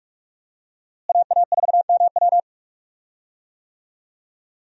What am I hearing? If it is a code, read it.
Morse code, heard as a keyed tone, about 46 wpm, 700 Hz. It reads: AA4MW
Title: USA Call Signs - 5 Characters - Repeated 2x - 45wpm